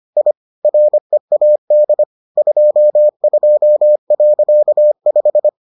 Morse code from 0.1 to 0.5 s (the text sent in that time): I